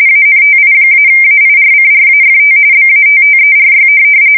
RTTY, about 45.45 baud. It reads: ANTI-AIRCRAFT BATTERY
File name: fsk170 / USS-Massachusetts